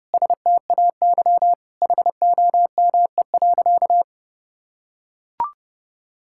Morse code, heard as a keyed tone, about 30 wpm, 700 Hz. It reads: STAY HOME.
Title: Sentences from Top 1000 Words - Review Part 6 - 30wpm